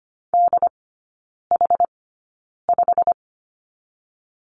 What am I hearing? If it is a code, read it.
Morse code, heard as a keyed tone, about 25 wpm, spaced as for 9 wpm, 700 Hz. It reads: DH5
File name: course / 25:10wpm / 093 Lesson 16e - Letter c - Three Characters 25wpm